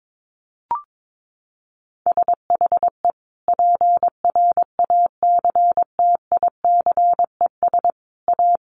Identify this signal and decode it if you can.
Morse code, heard as a keyed tone, about 22 wpm, 700 Hz. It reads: SHE PRACTICES A
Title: Additional Easy Phrases 22wpm